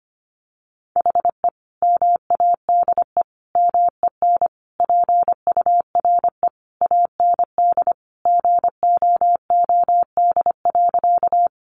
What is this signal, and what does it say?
Morse code, heard as a keyed tone, about 25 wpm, 700 Hz. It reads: HE MADE MEN PURE AND GOOD.
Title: Sentences from Top 1500 Words 25wpm